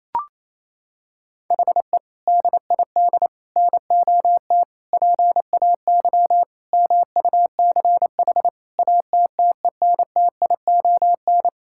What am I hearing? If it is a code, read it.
Morse code, heard as a keyed tone, about 27 wpm, 700 Hz. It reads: HE DID NOT PAY MUCH ATTENTION
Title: Sentences from Top 1000 Words - Review Part 6 - 28wpm